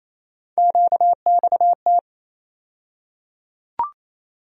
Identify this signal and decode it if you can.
Morse code, heard as a keyed tone, about 28 wpm, 700 Hz. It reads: QXT
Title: USA Call Sign Postfixes - 3 Letters 28wpm